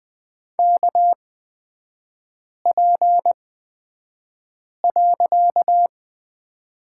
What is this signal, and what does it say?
Morse code, heard as a keyed tone, about 20 wpm, spaced as for 5 wpm, 690 Hz. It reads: KP.
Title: 255 Lesson 41e - Period - Three Characters 20wpm